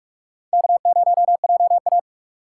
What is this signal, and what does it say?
Morse code, heard as a keyed tone, about 46 wpm, 700 Hz. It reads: K0JA
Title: USA Call Signs - 4 Characters - 8x Character Spacing - 45wpm